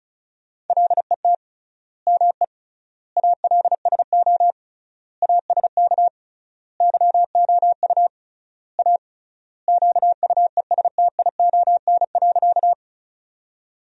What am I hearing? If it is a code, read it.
Morse code, heard as a keyed tone, about 36 wpm, 700 Hz. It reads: LET ME ALSO ASK YOU A QUESTION.
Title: 3x Word-Spacing - Sentences from Top 400 Words - 35wpm